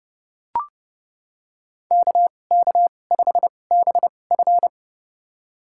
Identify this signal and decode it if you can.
Morse code, heard as a keyed tone, about 30 wpm, spaced as for 22 wpm, 700 Hz. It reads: KK5BF
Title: USA Call Signs - 5 Characters - 2x Character Spacing - 30wpm